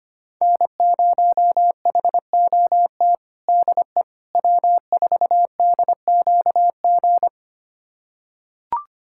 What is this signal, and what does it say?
Morse code, heard as a keyed tone, about 25 wpm, 700 Hz. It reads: N0HOT DE W4DQG
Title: USA Callsign DE Callsign - 5 Characters - 25wpm